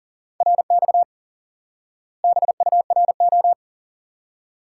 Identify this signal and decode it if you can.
Morse code, heard as a keyed tone, about 40 wpm, 700 Hz. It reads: RX BURO
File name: IQE Course 3 - Lesson 89 - YAGI - Two Elements - 40wpm